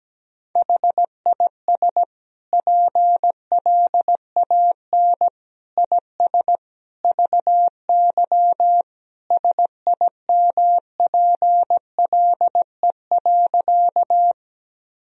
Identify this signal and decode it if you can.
Morse code, heard as a keyed tone, about 17 wpm, 700 Hz. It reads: HIS PLAN IS VY SIMPLE.
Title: Sentences from Top 500 Words and Abbreviations 17wpm